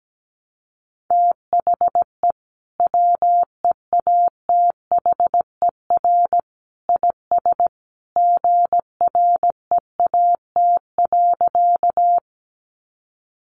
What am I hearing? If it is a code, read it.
Morse code, heard as a keyed tone, about 17 wpm, 700 Hz. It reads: THE WEATHER IS GREAT.
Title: Sentences from Top 800 Words 17wpm